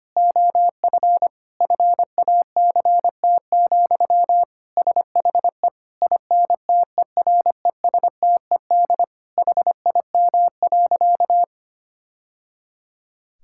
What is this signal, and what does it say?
Morse code, heard as a keyed tone, about 25 wpm, 700 Hz. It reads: OF FACT, SHE INTERESTED HIM.
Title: Sentences from Top 1000 Words - Review Part 6 - 25wpm